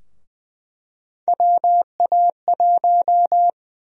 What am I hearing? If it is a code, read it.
Morse code, heard as a keyed tone, about 20 wpm, 700 Hz. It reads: WA1
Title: USA Call Sign Prefixes - 20wpm